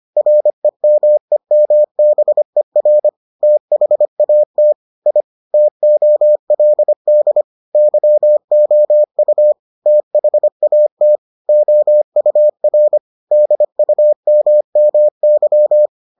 Morse code, heard as a keyed tone, about 25 wpm, 600 Hz. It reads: REMEMBER THAT I TOLD YOU THAT OUR DUMMY